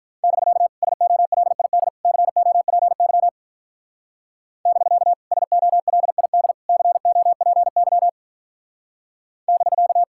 Morse code, heard as a keyed tone, about 46 wpm, 700 Hz. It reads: <BK> SOLID COPY <BK> SOLID COPY <BK>